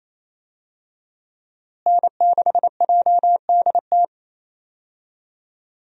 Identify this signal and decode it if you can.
Morse code, heard as a keyed tone, about 28 wpm, 700 Hz. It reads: N6JDT
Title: USA Call Signs - 5 Characters - Repeated 3x - 28wpm